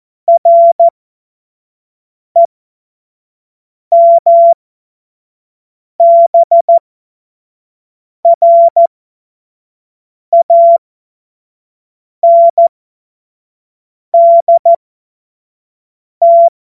Morse code, heard as a keyed tone, about 14 wpm, spaced as for 5 wpm, 680 Hz. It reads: REMBRANDT
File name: Morsekurs / Witze / Witz 23 Januar 2024 25 F70